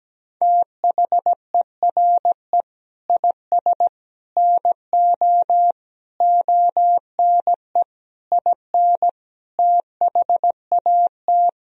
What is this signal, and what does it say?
Morse code, heard as a keyed tone, about 17 wpm, 700 Hz. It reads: THERE IS NO ONE IN THAT
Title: Sentences from Top 500 Words and Abbreviations 17wpm